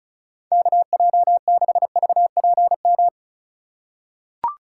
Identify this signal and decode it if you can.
Morse code, heard as a keyed tone, about 36 wpm, 700 Hz. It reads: KJ6VPM
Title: USA Call Signs - 6 Characters - 4x Character Spacing - 35wpm